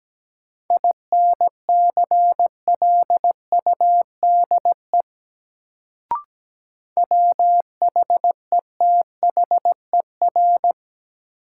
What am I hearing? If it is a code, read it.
Morse code, heard as a keyed tone, about 17 wpm, 700 Hz. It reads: INCLUDE WHETHER
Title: Sentences from Top 1000 Words - Review Part 3 - New Word Drill - 17wpm